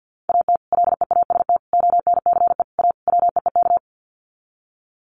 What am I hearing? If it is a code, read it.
Morse code, heard as a keyed tone, about 50 wpm, 700 Hz. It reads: AT LEAST ONCE A WEEK
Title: Sets of 5 Words Encore2 50wpm